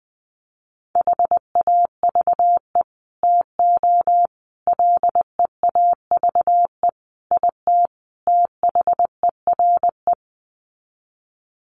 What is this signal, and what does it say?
Morse code, heard as a keyed tone, about 20 wpm, 700 Hz. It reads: HAVE TO LEAVE IT THERE
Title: Sets of 5 Words Encore2 20wpm